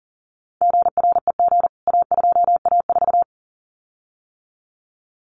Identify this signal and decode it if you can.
Morse code, heard as a keyed tone, about 40 wpm, 700 Hz. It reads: GREG A2A4
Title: CWT Exchange 40wpm